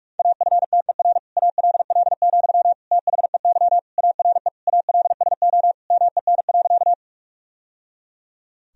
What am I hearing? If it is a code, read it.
Morse code, heard as a keyed tone, about 46 wpm, 700 Hz. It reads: AFTER ALL, THEY ARE ALSO MEN.
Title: Sentences from Top 1000 Words - Review Part 1 - 45wpm